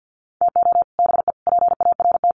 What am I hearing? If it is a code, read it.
Morse code, heard as a keyed tone, about 50 wpm, 700 Hz. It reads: TO BE PART
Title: Sets of 4 Words 50wpm